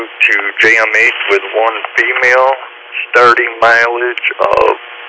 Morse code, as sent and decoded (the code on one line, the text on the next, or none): none